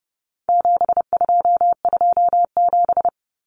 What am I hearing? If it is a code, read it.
Morse code, heard as a keyed tone, about 30 wpm, 700 Hz. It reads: 7227
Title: Common Cut Numbers - 4-Digit 30wpm